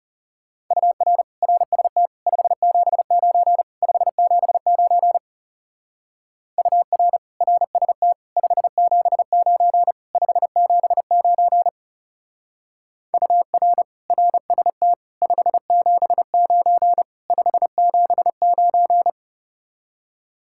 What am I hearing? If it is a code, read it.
Morse code, heard as a keyed tone, about 35 wpm, 700 Hz. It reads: UR RST 579 579 UR RST 579 579 UR RST 579 579
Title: Speed-Racing Common QSO Phrases - 30wpm